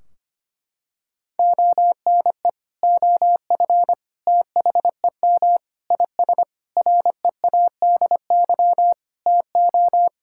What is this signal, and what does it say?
Morse code, heard as a keyed tone, about 25 wpm, 700 Hz. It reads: ONE OF THEM IS READY TO
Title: Sentences from Top 1600 Words 25wpm